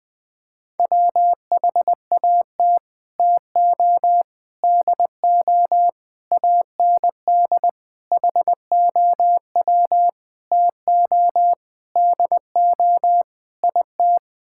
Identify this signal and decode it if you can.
Morse code, heard as a keyed tone, about 21 wpm, 700 Hz. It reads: WHAT TO DO AND HOW TO DO IT
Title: Sets of 8 Words - 20wpm